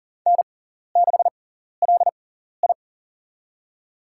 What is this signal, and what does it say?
Morse code, heard as a keyed tone, about 40 wpm, spaced as for 14 wpm, 700 Hz. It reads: N6LI
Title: USA Call Signs - 4 Characters - 6x Character Spacing - 40wpm